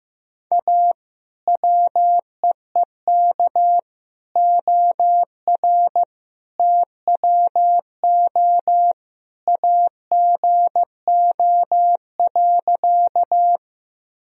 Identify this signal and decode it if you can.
Morse code, heard as a keyed tone, about 15 wpm, 700 Hz. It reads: A WEEK OR TWO AGO.
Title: Sentences from Top 1000 Words - Review Part 3 - 15wpm